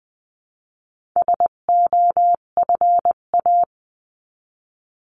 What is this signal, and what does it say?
Morse code, heard as a keyed tone, about 20 wpm, 700 Hz. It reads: SOFA
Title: 258 Lesson 42c - Words 16-20wpm